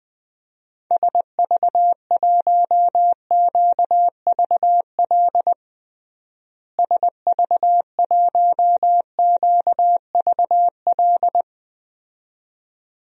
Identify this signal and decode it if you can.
Morse code, heard as a keyed tone, about 21 wpm, 700 Hz. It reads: SV1QVL SV1QVL
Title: DXCC-100 Call Signs - 6 Characters - Repeated 2x - 20wpm